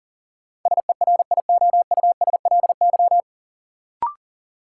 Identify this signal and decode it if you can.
Morse code, heard as a keyed tone, about 37 wpm, 700 Hz. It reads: SERIOUSLY
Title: Sentences from Top 1200 Words - New Word Drill Alt - 40wpm